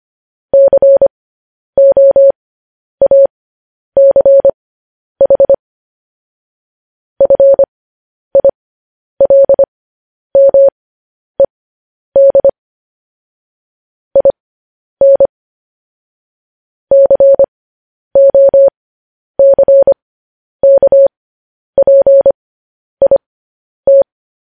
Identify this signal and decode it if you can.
Morse code, heard as a keyed tone, about 25 wpm, spaced as for 10 wpm, 550 Hz. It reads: COACH FILMED IN COCKPIT